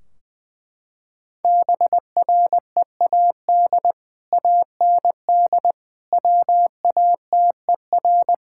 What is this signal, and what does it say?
Morse code, heard as a keyed tone, about 20 wpm, 700 Hz. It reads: BREAD AND WATER
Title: Sentences from Top 1600 Words 20wpm